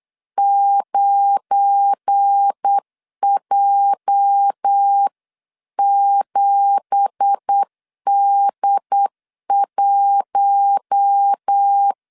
Morse code, heard as a keyed tone, about 8 wpm, 800 Hz. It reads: 9J7D1